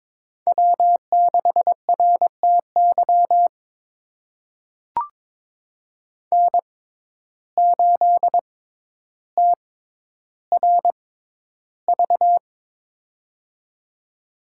Morse code, heard as a keyed tone, about 22 wpm, spaced as for 11 wpm, 700 Hz. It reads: W6RTY N8TRV
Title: USA Call Signs - 5 Characters - 6x Character Spacing - 22wpm